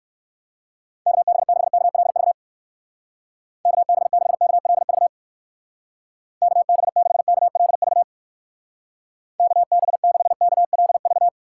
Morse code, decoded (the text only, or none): KB6KLV KB6KLV KB6KLV KB6KLV